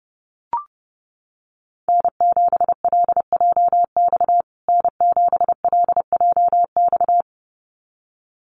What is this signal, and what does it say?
Morse code, heard as a keyed tone, about 30 wpm, 700 Hz. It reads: N7LJX N7LJX
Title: USA Call Signs Repeated 2x 30wpm